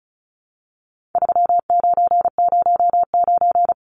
Morse code, heard as a keyed tone, about 34 wpm, 700 Hz. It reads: 3909
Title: Numbers - 4-Digit 35wpm